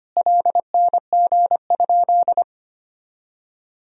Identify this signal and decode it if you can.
Morse code, heard as a keyed tone, about 25 wpm, 700 Hz. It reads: LNG?
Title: Sentences from Top 500 Words and Abbreviations 25wpm